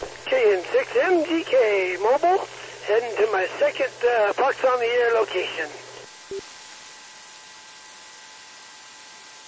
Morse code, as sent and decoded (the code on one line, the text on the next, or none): none